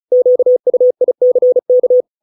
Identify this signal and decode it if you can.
Morse code, heard as a keyed tone, about 34 wpm, 500 Hz. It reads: QUICK